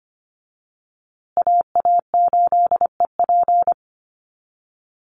Morse code, heard as a keyed tone, about 24 wpm, 700 Hz. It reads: AA8EP
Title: USA Call Signs Repeated 3x 25wpm